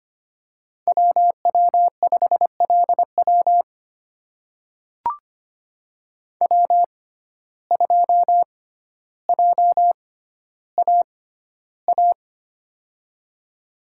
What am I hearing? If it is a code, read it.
Morse code, heard as a keyed tone, about 25 wpm, spaced as for 13 wpm, 700 Hz. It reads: WW5LW W2JAA
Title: USA Call Signs - 5 Characters - 6x Character Spacing - 25wpm